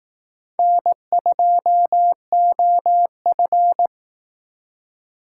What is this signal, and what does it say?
Morse code, heard as a keyed tone, about 18 wpm, 700 Hz. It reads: N2OF
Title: Speed Racing - USA 1x2 Call Signs 15wpm